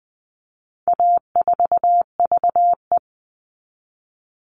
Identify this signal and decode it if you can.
Morse code, heard as a keyed tone, about 20 wpm, 700 Hz. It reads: A4VE
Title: All Cut Numbers - 4-Digit 20wpm